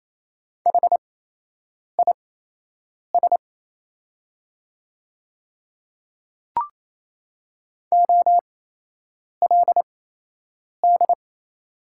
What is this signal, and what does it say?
Morse code, heard as a keyed tone, about 28 wpm, spaced as for 8 wpm, 690 Hz. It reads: HIS OLD
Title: 8x Character Spacing - Top 200 Words - 28wpm